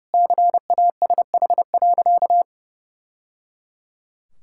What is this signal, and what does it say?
Morse code, heard as a keyed tone, about 30 wpm, 700 Hz. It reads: CASH.
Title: Sentences from Top 1000 Words - Review Part 5 - 30wpm